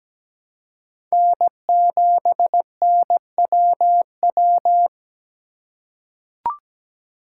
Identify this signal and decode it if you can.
Morse code, heard as a keyed tone, about 17 wpm, 700 Hz. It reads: N7NWW
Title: USA Call Signs - 5 Characters - 2x Character Spacing - 17wpm